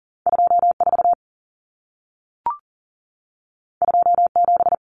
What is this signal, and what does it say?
Morse code, heard as a keyed tone, about 40 wpm, 700 Hz. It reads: 24 27
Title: Numbers - 2-Digit 40wpm